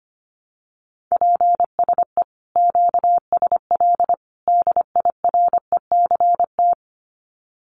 Morse code, heard as a keyed tone, about 25 wpm, 700 Hz. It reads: PSE QSL DIRECT
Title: Common QSO Phrases 25wpm